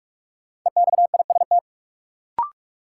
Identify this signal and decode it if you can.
Morse code, heard as a keyed tone, about 44 wpm, 700 Hz. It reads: EXIST
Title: Sentences from Top 1000 Words - Review Part 5 - New Word Drill - 45wpm